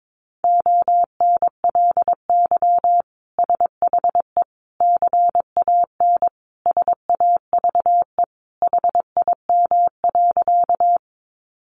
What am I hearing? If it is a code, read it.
Morse code, heard as a keyed tone, about 22 wpm, 700 Hz. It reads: ONLY SHE CAN SAVE HIM.
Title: Sentences from Top 1100 Words 22wpm